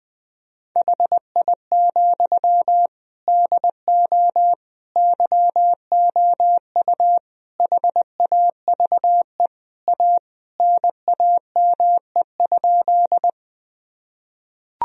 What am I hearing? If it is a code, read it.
Morse code, heard as a keyed tone, about 20 wpm, 700 Hz. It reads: HI, DO YOU HAVE A NAME?
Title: Sentences from Top 1000 Words - Review Part 3 - 20wpm